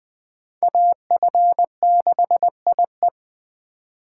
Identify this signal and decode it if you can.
Morse code, heard as a keyed tone, about 23 wpm, 700 Hz. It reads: AF6IE
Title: Speed Racing - USA 2x2 Call Signs 20wpm